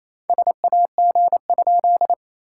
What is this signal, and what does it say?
Morse code, heard as a keyed tone, about 27 wpm, 700 Hz. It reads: SAG?
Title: SOTA QSO Practice 28wpm